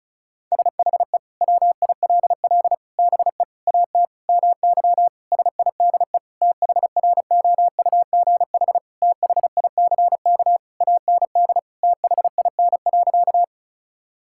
Morse code, heard as a keyed tone, about 36 wpm, 700 Hz. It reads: SHE WILL BE AT MY SIDE THROUGH THICK AND THIN.
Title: Sentences from Top 1600 Words 35wpm